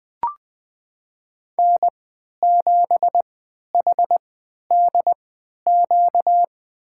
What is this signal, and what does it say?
Morse code, heard as a keyed tone, about 20 wpm, spaced as for 11 wpm, 700 Hz. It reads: N7HDQ
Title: USA Call Signs - 3x Character Spacing - 20wpm